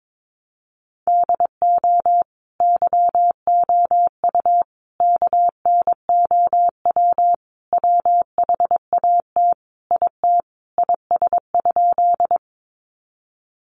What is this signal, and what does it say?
Morse code, heard as a keyed tone, about 22 wpm, 700 Hz. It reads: DO YOU KNOW WHAT IT IS?
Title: Sentences from Top 100 Words 22wpm